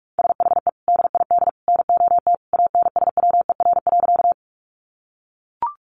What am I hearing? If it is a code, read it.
Morse code, heard as a keyed tone, about 44 wpm, 700 Hz. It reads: SHE DID NOT ANSWER.
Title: Sentences from Top 600 Words 45wpm